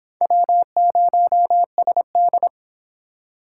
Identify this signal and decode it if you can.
Morse code, heard as a keyed tone, about 29 wpm, 700 Hz. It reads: W0SD